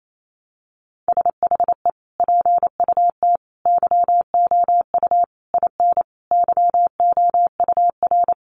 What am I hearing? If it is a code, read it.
Morse code, heard as a keyed tone, about 28 wpm, 700 Hz. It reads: SHE PUT YOU IN YOUR
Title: Sentences from Top 800 Words 28wpm